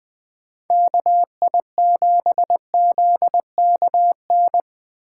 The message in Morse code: KI7ZKN

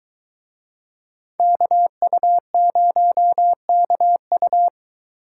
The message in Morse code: KU0KU

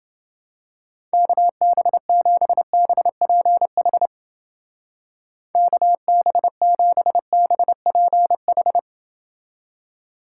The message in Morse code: KB7BPH KB7BPH

